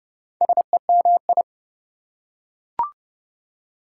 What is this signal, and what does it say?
Morse code, heard as a keyed tone, about 30 wpm, 700 Hz. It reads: SEMI
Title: Prefixes and Suffixes 30wpm